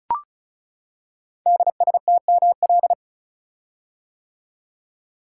Morse code, heard as a keyed tone, about 36 wpm, 690 Hz. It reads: DSTML
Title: Code Groups - 5 Letters - 35wpm